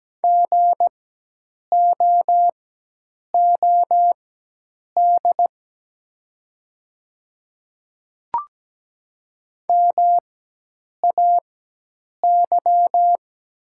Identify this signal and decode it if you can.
Morse code, heard as a keyed tone, about 17 wpm, spaced as for 8 wpm, 700 Hz. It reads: GOOD MAY
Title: 4x Character Spacing - Top 200 Words - 17wpm